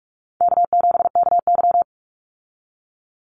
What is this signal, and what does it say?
Morse code, heard as a keyed tone, about 44 wpm, 700 Hz. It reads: K7KY
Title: Call Signs Repeated 3x - 45wpm